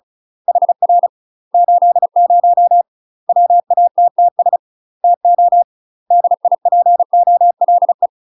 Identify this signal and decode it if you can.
Morse code, heard as a keyed tone, about 36 wpm, 700 Hz. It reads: HR 80 WATTS TO DIPOLE
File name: UK MASTER QSOs TRAINING / 35WPM / UK MASTER QSOs 0022